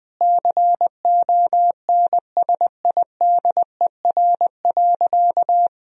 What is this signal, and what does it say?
Morse code, heard as a keyed tone, about 21 wpm, 700 Hz. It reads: CONSIDER.
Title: Sentences from Top 1000 Words - Review Part 4 - 20wpm